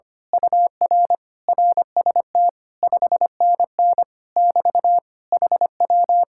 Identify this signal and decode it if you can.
Morse code, heard as a keyed tone, about 24 wpm, 700 Hz. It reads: UR RST 5NN <BT> HW
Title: Speed-Racing Common QSO Phrases - 25wpm